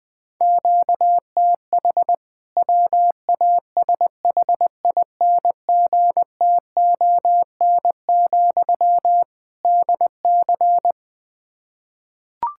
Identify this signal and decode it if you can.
Morse code, heard as a keyed tone, about 20 wpm, 700 Hz. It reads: QTH WASHINGTON, DC
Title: K1USN SST QSO Practice 20wpm